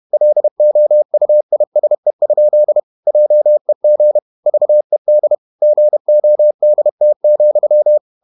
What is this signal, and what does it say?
Morse code, heard as a keyed tone, about 31 wpm, 600 Hz. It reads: LOUISE? JEG VED GODT,
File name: Ferie QRQ 31wpm